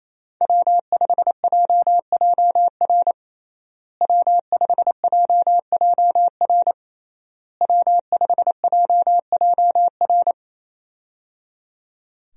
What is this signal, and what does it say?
Morse code, heard as a keyed tone, about 28 wpm, 700 Hz. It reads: W5JJR W5JJR W5JJR
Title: USA Call Signs - 5 Characters - Repeated 3x - 28wpm